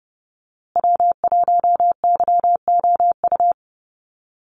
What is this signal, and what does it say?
Morse code, heard as a keyed tone, about 30 wpm, 700 Hz. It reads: W1YOU
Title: USA Call Signs Repeated 3x 30wpm